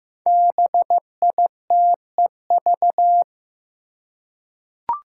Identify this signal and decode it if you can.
Morse code, heard as a keyed tone, about 15 wpm, 700 Hz. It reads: BITEV
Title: Five Challenging Letter-Number Combinations 15wpm